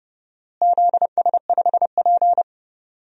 Morse code, heard as a keyed tone, about 30 wpm, 700 Hz. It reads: ZS5P
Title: DXCC-100 Call Signs - 4 Characters - Repeated 2x - 30wpm